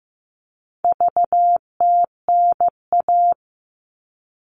Morse code, heard as a keyed tone, about 15 wpm, 700 Hz. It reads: VTNA
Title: All Cut Numbers - 4-Digit 15wpm